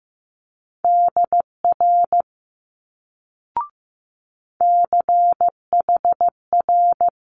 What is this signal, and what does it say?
Morse code, heard as a keyed tone, about 15 wpm, 700 Hz. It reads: DR CHR
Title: Multi-letter Phonemes Quick Fire 15wpm